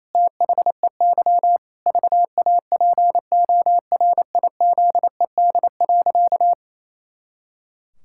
Morse code, heard as a keyed tone, about 28 wpm, 700 Hz. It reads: THEY VAPORIZED.
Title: Pangram Challenge v2 - 28wpm